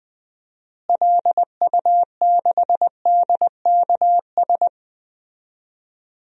Morse code, heard as a keyed tone, about 20 wpm, 700 Hz. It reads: LU6DKS
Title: DXCC-100 Call Signs - 6 Characters - Repeated 3x - 20wpm